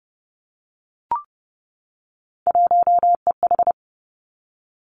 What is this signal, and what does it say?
Morse code, heard as a keyed tone, about 30 wpm, 700 Hz. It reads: 1EH
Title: Three Challenging Letter-Number Combinations 30wpm